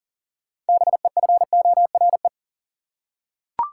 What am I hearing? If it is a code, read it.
Morse code, heard as a keyed tone, about 40 wpm, 700 Hz. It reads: BEFORE
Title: Six Letter Words - 40wpm